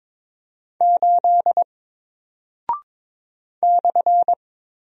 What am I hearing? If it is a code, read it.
Morse code, heard as a keyed tone, about 22 wpm, 700 Hz. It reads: 8 /
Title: Single Character (Letter:Number:Punctuation:Prosign) Alt 22wpm